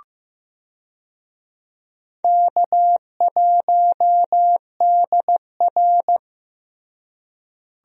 Morse code, heard as a keyed tone, about 15 wpm, 700 Hz. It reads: K1DR
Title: USA Call Signs - 4 Characters - Repeated 2x - 15wpm